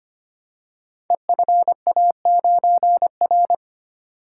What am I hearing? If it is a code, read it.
Morse code, heard as a keyed tone, about 25 wpm, 700 Hz. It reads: EFA9R
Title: Code Groups - 5 Letter:Numbers - 25wpm